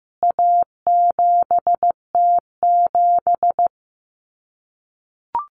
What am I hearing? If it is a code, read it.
Morse code, heard as a keyed tone, about 15 wpm, 700 Hz. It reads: A7T7
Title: CWT Exchange 15wpm